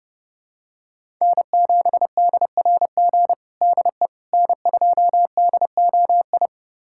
Morse code, heard as a keyed tone, about 30 wpm, 700 Hz. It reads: N7DRG DE N2DOI
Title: USA Callsign DE Callsign - 5 Characters - 30wpm